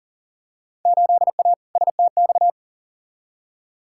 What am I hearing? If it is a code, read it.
Morse code, heard as a keyed tone, about 40 wpm, 700 Hz. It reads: 8A STX
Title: Field Day Exchanges Encore - 40wpm